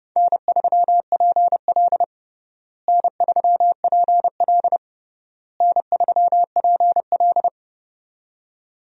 Morse code, heard as a keyed tone, about 30 wpm, 700 Hz. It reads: N3PL N3PL N3PL
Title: USA Call Signs - 4 Characters - Repeated 3x - 30wpm